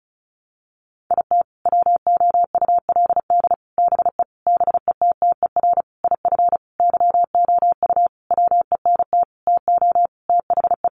Morse code, heard as a keyed tone, about 34 wpm, 700 Hz. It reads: IT WOULD BE BETTER IF YOU WENT TO THE
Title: Sentences from Top 400 Words 35wpm